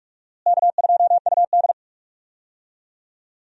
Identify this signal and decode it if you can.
Morse code, heard as a keyed tone, about 46 wpm, 700 Hz. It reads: K2UD
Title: USA Call Signs - 4 Characters - Repeated 2x - 45wpm